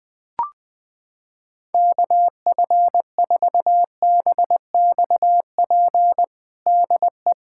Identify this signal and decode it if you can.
Morse code, heard as a keyed tone, about 20 wpm, 700 Hz. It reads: KF4BXP DE
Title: USA Callsign DE Callsign - 6 Characters - 20wpm